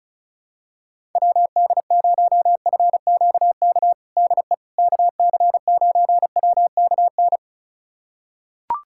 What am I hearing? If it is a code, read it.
Morse code, heard as a keyed tone, about 36 wpm, 700 Hz. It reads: WD0FQK DE KC9WKN
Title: USA Callsign DE Callsign 35wpm